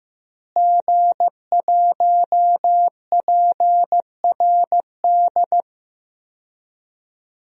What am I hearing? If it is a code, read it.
Morse code, heard as a keyed tone, about 16 wpm, 700 Hz. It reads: G1PRD
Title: DXCC-100 Call Signs - 5 Characters - Repeated 2x - 15wpm